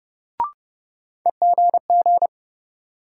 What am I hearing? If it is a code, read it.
Morse code, heard as a keyed tone, about 29 wpm, 700 Hz. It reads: EGG
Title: Sentences from Top 1000 Words - Review Part 7 - New Word Drill - 30wpm